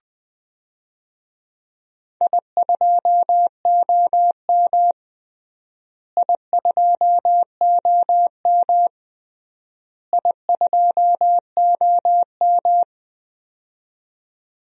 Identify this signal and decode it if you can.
Morse code, heard as a keyed tone, about 20 wpm, 700 Hz. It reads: I2OM I2OM I2OM
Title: DXCC-100 Call Signs - 4 Characters - Repeated 3x - 20wpm